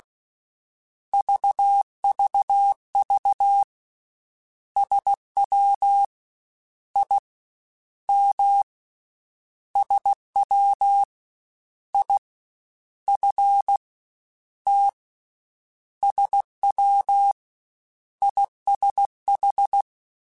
Morse code, heard as a keyed tone, about 16 wpm, 800 Hz. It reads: VVV SW I M SW I F T SW ISH
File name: Misc / Misc-mp3s / Jumps @15